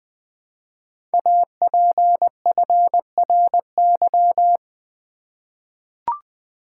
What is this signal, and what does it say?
Morse code, heard as a keyed tone, about 20 wpm, 700 Hz. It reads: APFRY
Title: Code Groups - 5 Letters - 20wpm